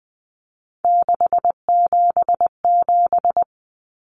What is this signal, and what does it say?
Morse code, heard as a keyed tone, about 20 wpm, 700 Hz. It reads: 677
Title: Common Cut Numbers - 3-Digit 20wpm